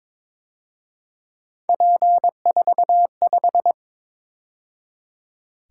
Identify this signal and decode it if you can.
Morse code, heard as a keyed tone, about 26 wpm, 700 Hz. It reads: P45
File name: Speed-Racing - Three Letter-Number Combinations - 22wpm